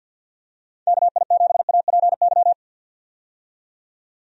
Morse code, heard as a keyed tone, about 51 wpm, 700 Hz. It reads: KI7APY
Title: USA Call Signs - 6 Characters - 50wpm